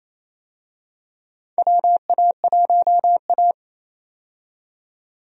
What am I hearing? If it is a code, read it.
Morse code, heard as a keyed tone, about 27 wpm, 700 Hz. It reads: WA1A
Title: USA Call Signs - 4 Characters - 28wpm